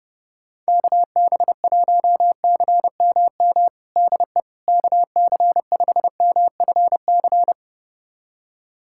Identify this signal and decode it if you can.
Morse code, heard as a keyed tone, about 30 wpm, 700 Hz. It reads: KB1CMM DE KC5MFC
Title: USA Callsign DE Callsign - 6 Characters - 30wpm